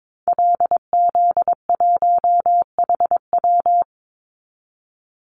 Morse code, heard as a keyed tone, about 22 wpm, 700 Hz. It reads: LZ1HW
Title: Call Signs Repeated 3x - 22wpm